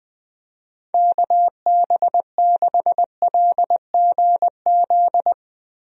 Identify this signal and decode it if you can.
Morse code, heard as a keyed tone, about 20 wpm, 700 Hz. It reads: KB6LGZ
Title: USA Call Signs - 6 Characters - Repeated 2x - 20wpm